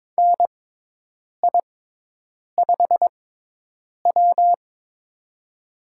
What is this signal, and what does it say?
Morse code, heard as a keyed tone, about 22 wpm, spaced as for 8 wpm, 700 Hz. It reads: NI5W
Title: USA Call Signs - 4 Characters - 6x Character Spacing - 22wpm